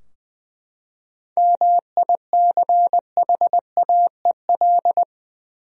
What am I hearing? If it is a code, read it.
Morse code, heard as a keyed tone, about 20 wpm, 700 Hz. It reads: MICHAEL
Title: USA Common Names - 20wpm